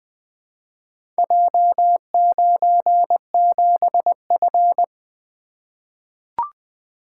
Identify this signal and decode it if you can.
Morse code, heard as a keyed tone, about 20 wpm, 700 Hz. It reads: J97F
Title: Four Letter-Number Combinations - 20wpm